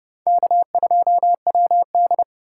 Morse code, heard as a keyed tone, about 27 wpm, 700 Hz. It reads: K2WD